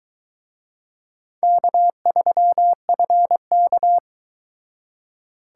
Morse code, heard as a keyed tone, about 23 wpm, 700 Hz. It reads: K3FK